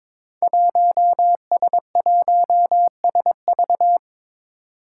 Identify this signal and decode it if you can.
Morse code, heard as a keyed tone, about 22 wpm, 700 Hz. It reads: 1S1SV
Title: Five Challenging Letter-Number Combinations 22wpm